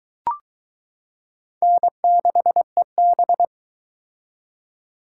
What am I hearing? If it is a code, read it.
Morse code, heard as a keyed tone, about 21 wpm, 700 Hz. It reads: N6EB